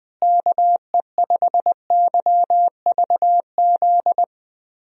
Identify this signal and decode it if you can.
Morse code, heard as a keyed tone, about 20 wpm, 700 Hz. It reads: KE5YVZ
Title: USA Call Signs - 6 Characters - Repeated 3x - 20wpm